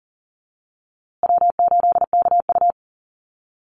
Morse code, heard as a keyed tone, about 39 wpm, 700 Hz. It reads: W8KU
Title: USA Call Signs Repeated 2x 40wpm